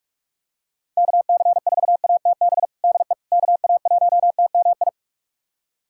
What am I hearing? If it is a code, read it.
Morse code, heard as a keyed tone, about 46 wpm, 700 Hz. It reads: KC4ATB DE KA1TMI
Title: USA Callsign DE Callsign - 6 Characters - 45wpm